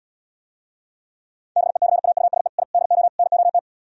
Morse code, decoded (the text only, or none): BECAUSE I DO NOT